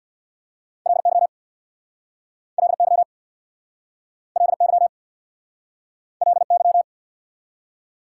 Morse code, heard as a keyed tone, about 59 wpm, 690 Hz. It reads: LY LY LY LY
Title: Speed-Racing Prefixes and Suffixes 50wpm